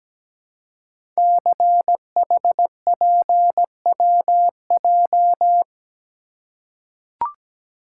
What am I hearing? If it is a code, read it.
Morse code, heard as a keyed tone, about 17 wpm, 700 Hz. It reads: CHPWJ
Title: Code Groups - 5 Letters - 17wpm